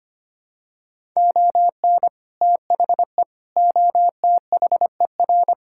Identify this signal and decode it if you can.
Morse code, heard as a keyed tone, about 25 wpm, 700 Hz. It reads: ON THE OTHER
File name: practice / 25wpm / Sentences with 4-Word Phrases - 25wpm